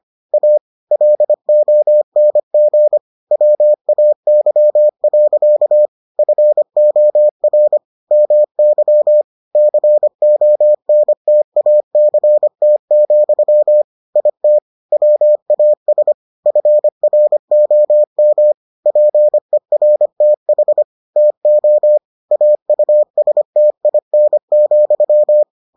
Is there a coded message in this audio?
Yes: Morse code from 0.3 to 25.4 s, heard as a keyed tone, about 25 wpm, 600 Hz. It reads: A LONG WAY. FOR MY CONTACT, IT WAS FROM PERTH TO AUSTIN,